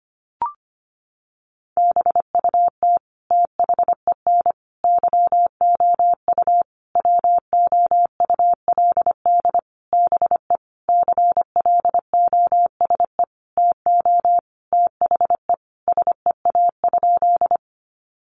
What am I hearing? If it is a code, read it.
Morse code, heard as a keyed tone, about 25 wpm, 700 Hz. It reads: BUT THEN YOU WOULD BE CLOSE TO THE SEA?
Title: Sentences from Top 300 Words 25wpm